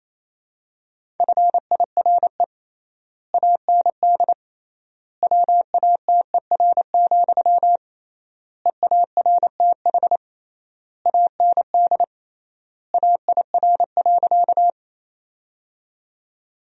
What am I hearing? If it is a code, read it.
Morse code, heard as a keyed tone, about 28 wpm, 700 Hz. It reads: FIRE AND WATER, EARTH AND AIR.
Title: 3x Word-Spacing - Sentences from Top 400 Words - 28wpm